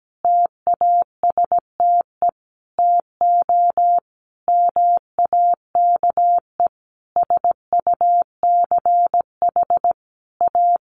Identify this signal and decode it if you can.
Morse code, heard as a keyed tone, about 17 wpm, 700 Hz. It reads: TASTE TO MAKE SUCH A
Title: Sentences from Top 1200 Words 17wpm